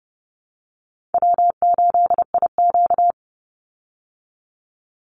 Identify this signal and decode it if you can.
Morse code, heard as a keyed tone, about 30 wpm, 700 Hz. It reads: W8IQ
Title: Call Signs Repeated 3x - 30wpm